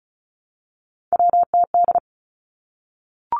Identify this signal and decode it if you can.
Morse code, heard as a keyed tone, about 36 wpm, 700 Hz. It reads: WTD
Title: Three Challenging Letter-Number Combinations 35wpm